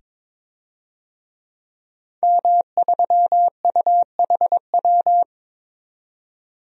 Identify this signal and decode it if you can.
Morse code, heard as a keyed tone, about 22 wpm, 700 Hz. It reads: M3UHW
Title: DXCC-100 Call Signs - 5 Characters - 22wpm